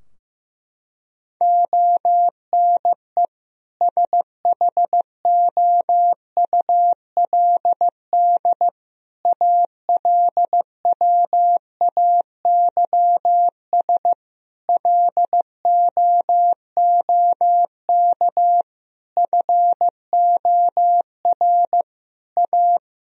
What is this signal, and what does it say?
Morse code, heard as a keyed tone, about 16 wpm, 700 Hz. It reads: ONE SHOULD ALWAYS LOOK FOR A
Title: Sentences from Top 1000 Words - Review Part 4 - 15wpm